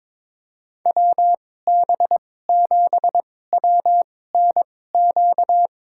Morse code, heard as a keyed tone, about 22 wpm, spaced as for 16 wpm, 700 Hz. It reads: WB7WNQ
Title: USA Call Signs - 2x Character Spacing - 22wpm